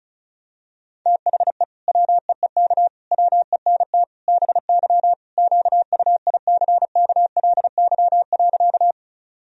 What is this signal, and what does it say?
Morse code, heard as a keyed tone, about 34 wpm, 700 Hz. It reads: THE WEEK WENT BY QUICKLY.
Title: Sentences from Top 1000 Words - Review Part 2 - 35wpm